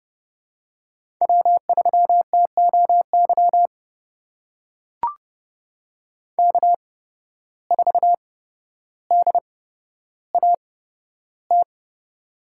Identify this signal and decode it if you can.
Morse code, heard as a keyed tone, about 30 wpm, spaced as for 13 wpm, 700 Hz. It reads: W3TOY K4DAT
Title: USA Call Signs - 5 Characters - 8x Character Spacing - 30wpm